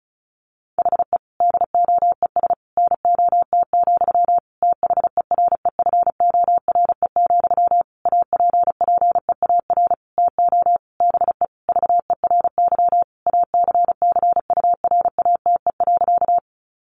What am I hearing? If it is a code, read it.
Morse code, heard as a keyed tone, about 34 wpm, 700 Hz. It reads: HE DOES NOT, THEREFORE, APPEAR TO BE VERY ACCURATE.
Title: Sentences from Top 1200 Words 35wpm